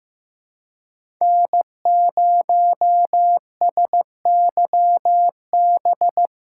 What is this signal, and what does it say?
Morse code, heard as a keyed tone, about 15 wpm, 700 Hz. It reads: N0SYB
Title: USA Callsign DE Callsign - 5 Characters - 15wpm